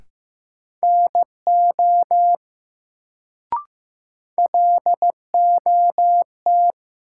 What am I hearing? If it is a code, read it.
Morse code, heard as a keyed tone, about 15 wpm, 700 Hz. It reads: NO LOT
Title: Sets of 2 Words - New Word Drill - 15wpm